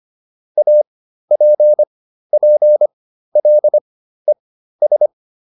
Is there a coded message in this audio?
Yes: Morse code from 0.6 to 5.1 s, heard as a keyed tone, about 25 wpm, spaced as for 13 wpm, 600 Hz. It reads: APPLES